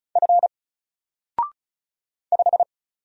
Morse code, heard as a keyed tone, about 34 wpm, 700 Hz. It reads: F 5
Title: Single Character (Letter:Number:Punctuation:Prosign) 35wpm